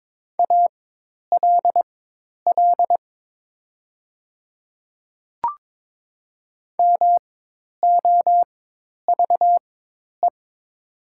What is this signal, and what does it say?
Morse code, heard as a keyed tone, about 22 wpm, spaced as for 10 wpm, 700 Hz. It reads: ALL MOVE
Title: 4x Character Spacing - Top 200 Words - 22wpm